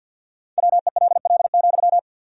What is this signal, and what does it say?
Morse code, heard as a keyed tone, about 50 wpm, 700 Hz. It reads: WELL,
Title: 4x Word-Spacing - Sentences from Top 300 Words - 50wpm